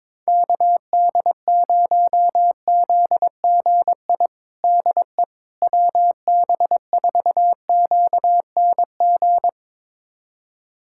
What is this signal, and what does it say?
Morse code, heard as a keyed tone, about 21 wpm, 700 Hz. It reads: KD0ZGI DE WB4QNG
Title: USA Callsign DE Callsign - 6 Characters - 22wpm